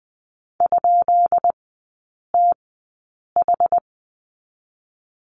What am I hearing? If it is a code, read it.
Morse code, heard as a keyed tone, about 20 wpm, spaced as for 8 wpm, 690 Hz. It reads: ?TH
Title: 166 Lesson 27e - Number 7 - Three Characters 20wpm